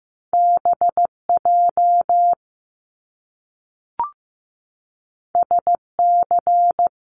Morse code, heard as a keyed tone, about 15 wpm, 700 Hz. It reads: BJ SC
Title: Top Two Letter Combinations in Words 15wpm